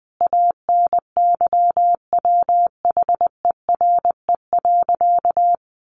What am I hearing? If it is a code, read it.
Morse code, heard as a keyed tone, about 20 wpm, 700 Hz. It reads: ANYWHERE.
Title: Sentences from Top 1400 Words 20wpm